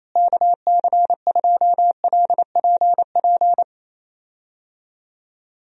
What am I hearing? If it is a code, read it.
Morse code, heard as a keyed tone, about 28 wpm, 700 Hz. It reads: KC2LPP
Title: USA Call Signs - 6 Characters - Repeated 3x - 28wpm